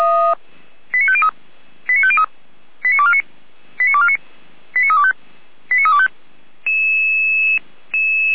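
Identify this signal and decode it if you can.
Five-tone selective call (ZVEI), heard as quick runs of five tones, tones of about 70 ms. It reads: 89503 89503 89249 89249 89325 89325
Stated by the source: ZVEI five-tone selective call received by SDR, from feuerwehr-orlamuende.org fire brigade archive.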